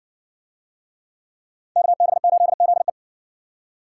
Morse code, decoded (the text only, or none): KB8ZE